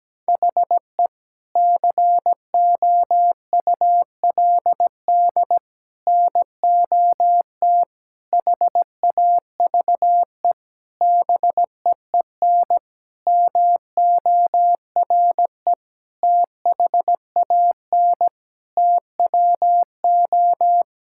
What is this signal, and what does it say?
Morse code, heard as a keyed tone, about 17 wpm, 700 Hz. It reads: HE COULD NOT HAVE BEEN MORE THAN TWO
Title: Sentences from Top 1600 Words 17wpm